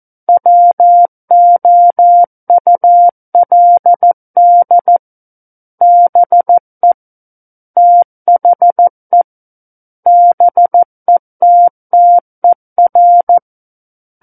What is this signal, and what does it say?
Morse code, heard as a keyed tone, about 14 wpm, 700 Hz. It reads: WOULD BE THE BETTER
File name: Book One / 05-Chapter 5 - 14 WPM